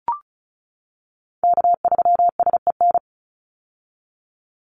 Morse code, heard as a keyed tone, about 34 wpm, 700 Hz. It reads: K3SEN
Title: Call Signs - 35wpm